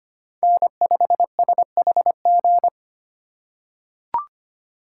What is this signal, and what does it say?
Morse code, heard as a keyed tone, about 25 wpm, 700 Hz. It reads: N5SHG
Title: USA Call Signs - 5 Characters - 2x Character Spacing - 25wpm